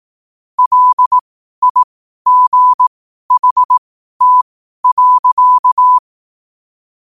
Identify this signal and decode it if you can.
Morse code, heard as a keyed tone, about 18 wpm, spaced as for 12 wpm, 1000 Hz. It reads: LIGHT.
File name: qotdfiles / 12wpm